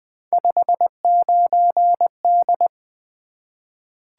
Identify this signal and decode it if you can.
Morse code, heard as a keyed tone, about 20 wpm, 700 Hz. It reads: 59D
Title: Three Letter-Number Combinations - 20wpm